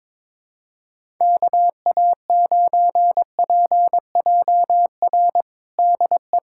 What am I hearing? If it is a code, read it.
Morse code, heard as a keyed tone, about 22 wpm, 700 Hz. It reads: KA9PJR DE
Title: USA Callsign DE Callsign 22wpm